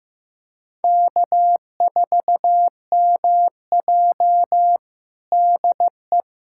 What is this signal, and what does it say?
Morse code, heard as a keyed tone, about 15 wpm, 700 Hz. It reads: K4MJ DE
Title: USA Callsign DE Callsign - 4 Characters - 15wpm